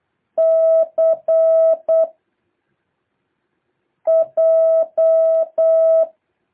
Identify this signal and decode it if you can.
Morse code, heard as a keyed tone, about 8 wpm, 650 Hz. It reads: C J
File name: jh8cbh / 2023cw